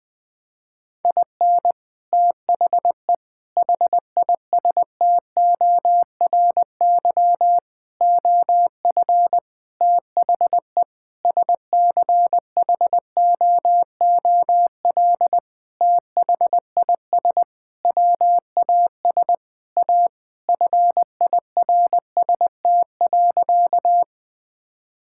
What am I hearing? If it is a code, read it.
Morse code, heard as a keyed tone, about 21 wpm, 700 Hz. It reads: IN THE HISTORY OF THE SCHOOL THIS WAS A FIRST.
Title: Sentences with 5-Word Phrases - 20wpm